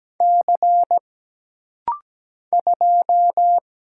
Morse code, heard as a keyed tone, about 17 wpm, 700 Hz. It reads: C 2
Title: Single Letter-Number - Rapid-Fire 17wpm